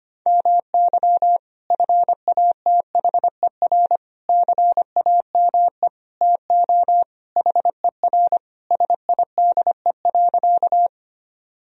Morse code, heard as a keyed tone, about 25 wpm, 700 Hz. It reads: MY FATHER CAME TO HER SIDE.
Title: Sentences from Top 1000 Words - Review Part 1 - 25wpm